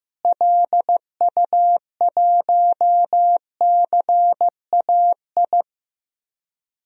Morse code, heard as a keyed tone, about 15 wpm, 700 Hz. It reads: LU1CAI
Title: DXCC-100 Call Signs - 6 Characters - Repeated 3x - 15wpm